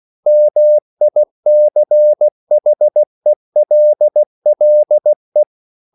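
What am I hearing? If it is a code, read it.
Morse code, heard as a keyed tone, about 16 wpm, 600 Hz. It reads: MICHELLE
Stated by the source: Slow (QRS) Morse code at 16 wpm, from CW med Gnister.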